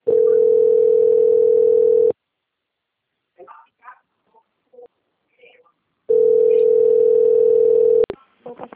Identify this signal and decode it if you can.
Ringback tone (North America), 440 + 480 Hz, with speech.
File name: out- - - - - .6259